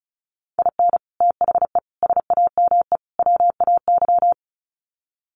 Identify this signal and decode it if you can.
Morse code, heard as a keyed tone, about 34 wpm, 700 Hz. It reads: IN THE SAME WAY
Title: Sets of 4 Words 35wpm